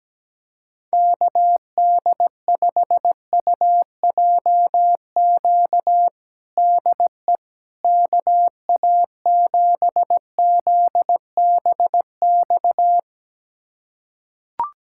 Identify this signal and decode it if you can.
Morse code, heard as a keyed tone, about 17 wpm, 700 Hz. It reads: KD5UJQ DE KA7ZBX
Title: USA Callsign DE Callsign - 6 Characters - 17wpm